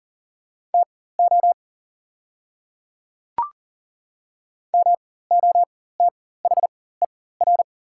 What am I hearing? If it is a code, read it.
Morse code, heard as a keyed tone, about 40 wpm, spaced as for 19 wpm, 690 Hz. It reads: TO MOTHER
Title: 4x Character Spacing - Top 300 Words - 40wpm